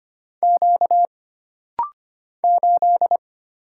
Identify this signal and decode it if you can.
Morse code, heard as a keyed tone, about 25 wpm, 700 Hz. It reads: Q 8
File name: Single Character (Letter:Number:Punctuation:Prosign) - Rapid-Fire - 25wpm